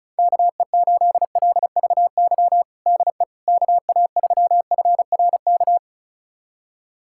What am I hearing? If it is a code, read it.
Morse code, heard as a keyed tone, about 36 wpm, 700 Hz. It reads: KE8LVY DE KA3FRK
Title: USA Callsign DE Callsign - 6 Characters - 35wpm